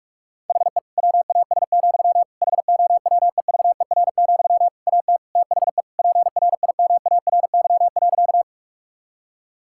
Morse code, heard as a keyed tone, about 46 wpm, 700 Hz. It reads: HE WAS, HOWEVER, AT THE PRIMARY.
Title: Sentences from Top 1000 Words - Review Part 5 - 45wpm